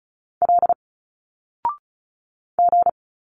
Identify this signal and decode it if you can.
Morse code, heard as a keyed tone, about 36 wpm, 700 Hz. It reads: L G
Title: Single Letters 35wpm